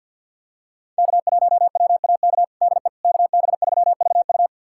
Morse code, heard as a keyed tone, about 52 wpm, 700 Hz. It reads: K1WAK DE KB3VU
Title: USA Callsign DE Callsign - 5 Characters - 50wpm